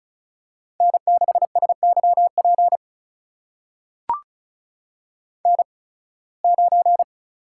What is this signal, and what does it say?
Morse code, heard as a keyed tone, about 36 wpm, spaced as for 23 wpm, 700 Hz. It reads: N6SYP N9
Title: USA Call Signs - 5 Characters - 8x Character Spacing - 35wpm